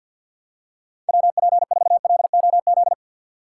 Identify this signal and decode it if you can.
Morse code, heard as a keyed tone, about 49 wpm, 700 Hz. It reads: WP4LOZ
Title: USA Call Signs - 6 Characters - 3x Character Spacing - 50wpm